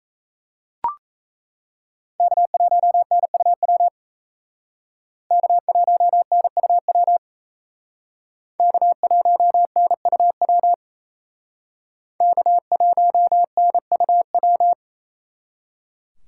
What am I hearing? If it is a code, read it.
Morse code, heard as a keyed tone, about 35 wpm, 700 Hz. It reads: K1NUW K1NUW K1NUW K1NUW